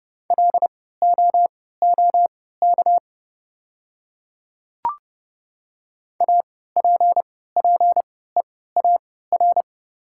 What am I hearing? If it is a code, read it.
Morse code, heard as a keyed tone, about 29 wpm, spaced as for 17 wpm, 690 Hz. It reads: LOOK APPEAR
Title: 3x Character Spacing - Top 500 Words - 30wpm